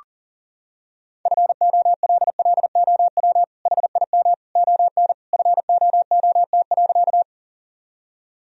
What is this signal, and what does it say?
Morse code, heard as a keyed tone, about 40 wpm, 700 Hz. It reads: FOLLOW HIM ON FOOT.
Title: Sentences from Top 1000 Words - Review Part 3 - 40wpm